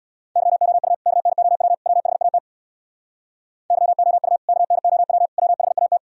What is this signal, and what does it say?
Morse code, heard as a keyed tone, about 64 wpm, 700 Hz. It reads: YOU KNOW WHAT YOU KNOW WHAT